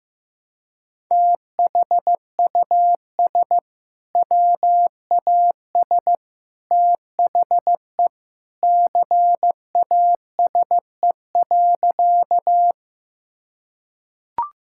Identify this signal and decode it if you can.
Morse code, heard as a keyed tone, about 15 wpm, 700 Hz. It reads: THUS WAS THE CASE.
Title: Sentences from Top 1000 Words - Review Part 5 - 15wpm